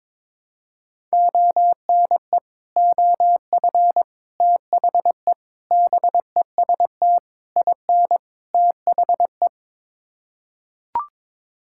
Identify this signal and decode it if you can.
Morse code, heard as a keyed tone, about 22 wpm, 700 Hz. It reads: ONE OF THE BEST IN THE
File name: practice / 22wpm / Sets of 6 Words - 22wpm